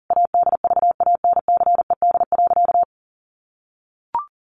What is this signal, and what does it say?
Morse code, heard as a keyed tone, about 40 wpm, 700 Hz. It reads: ADVANCED.
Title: Sentences from Top 1400 Words 40wpm